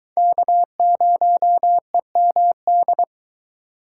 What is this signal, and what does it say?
Morse code, heard as a keyed tone, about 20 wpm, 700 Hz. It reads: K0EMD